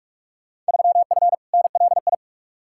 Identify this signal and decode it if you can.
Morse code, heard as a keyed tone, about 45 wpm, 700 Hz. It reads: 3F NLI
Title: Field Day Exchanges - 45wpm